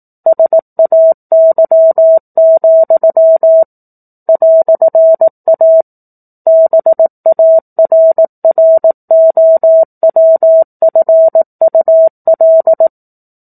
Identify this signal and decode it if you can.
Morse code, heard as a keyed tone, about 18 wpm, 650 Hz. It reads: SAY, "A BARROWFUL
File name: 04-Chapter-04 18 WPM